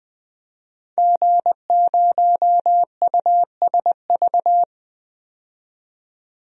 Morse code, heard as a keyed tone, about 20 wpm, 700 Hz. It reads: G0USV
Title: DXCC-100 Call Signs - 5 Characters - 20wpm